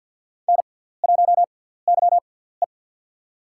Morse code, heard as a keyed tone, about 49 wpm, spaced as for 17 wpm, 700 Hz. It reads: N1YE
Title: USA Call Signs - 4 Characters - 6x Character Spacing - 50wpm